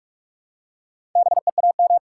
Morse code, heard as a keyed tone, about 46 wpm, 700 Hz. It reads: BEAM
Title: Top 90 QSO Elements 45wpm